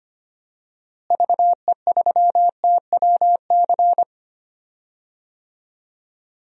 Morse code, heard as a keyed tone, about 25 wpm, 700 Hz. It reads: VE3TWC
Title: DXCC-100 Call Signs - 6 Characters - Repeated 3x - 25wpm